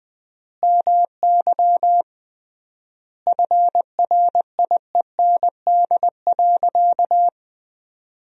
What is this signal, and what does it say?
Morse code, heard as a keyed tone, about 20 wpm, 700 Hz. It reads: MY FRIEND.
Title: 3x Word-Spacing - Sentences from Top 300 Words - 20wpm